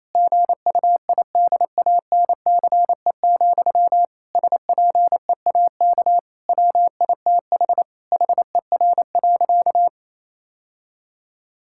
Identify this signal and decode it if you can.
Morse code, heard as a keyed tone, about 28 wpm, 700 Hz. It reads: GUIDANCE, SPEAK WITH HER.
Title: Sentences from Top 1600 Words 28wpm